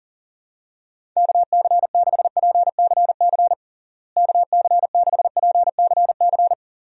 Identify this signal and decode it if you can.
Morse code, heard as a keyed tone, about 40 wpm, 700 Hz. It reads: KC6PCC KC6PCC
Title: USA Call Signs - 6 Characters - Repeated 3x - 40wpm